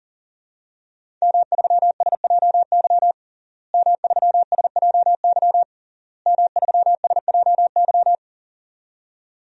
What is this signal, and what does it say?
Morse code, heard as a keyed tone, about 40 wpm, 700 Hz. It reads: M3SJY M3SJY M3SJY
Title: DXCC-100 Call Signs - 5 Characters - Repeated 3x - 40wpm